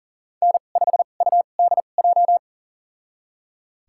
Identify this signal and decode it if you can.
Morse code, heard as a keyed tone, about 40 wpm, spaced as for 29 wpm, 700 Hz. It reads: N5UDJ
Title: USA Call Signs - 5 Characters - 2x Character Spacing - 40wpm